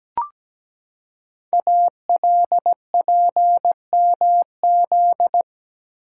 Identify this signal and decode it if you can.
Morse code, heard as a keyed tone, about 17 wpm, 700 Hz. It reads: ALPMZ
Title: Code Groups - 5 Letter:Numbers - 17wpm